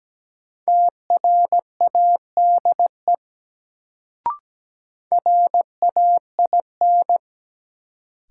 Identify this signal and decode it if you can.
Morse code, heard as a keyed tone, about 17 wpm, 700 Hz. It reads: TRADE RAIN
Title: Sentences from Top 600 Words - New Word Drill - 17wpm